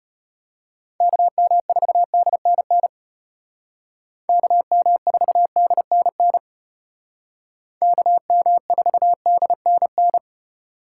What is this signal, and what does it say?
Morse code, heard as a keyed tone, about 32 wpm, 700 Hz. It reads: KM4DNN KM4DNN KM4DNN